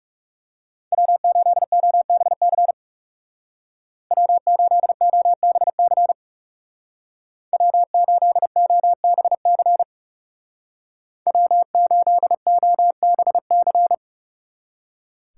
Morse code, heard as a keyed tone, about 39 wpm, 700 Hz. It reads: W8OBC W8OBC W8OBC W8OBC